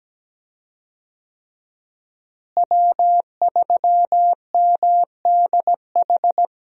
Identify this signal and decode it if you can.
Morse code, heard as a keyed tone, about 17 wpm, 700 Hz. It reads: W3MDH
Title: USA Call Signs - 5 Characters - Repeated 3x - 17wpm